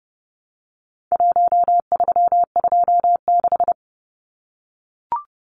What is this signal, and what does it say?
Morse code, heard as a keyed tone, about 30 wpm, 700 Hz. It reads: 1326
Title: Numbers - Contest Style - 4-Digit 30wpm